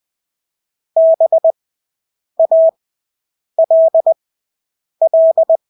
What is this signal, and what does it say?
Morse code, heard as a keyed tone, about 20 wpm, spaced as for 8 wpm, 650 Hz. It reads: BALL
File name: Engelske ord 20/8 wpm